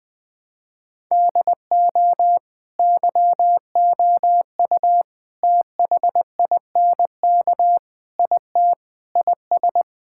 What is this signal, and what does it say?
Morse code, heard as a keyed tone, about 20 wpm, 700 Hz. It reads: DO YOU THINK IT IS
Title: Sentences from Top 1000 Words - Review Part 6 - 20wpm